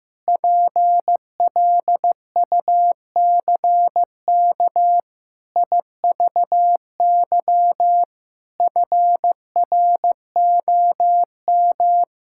Morse code, heard as a keyed tone, about 15 wpm, 700 Hz. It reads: PLUCK IVY FROM
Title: Pangram Challenge v2 - 15wpm